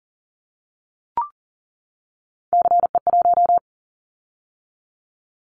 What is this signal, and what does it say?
Morse code, heard as a keyed tone, about 42 wpm, 700 Hz. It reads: CE1
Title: Callsign Prefixes for Top 100 DXCC - 40wpm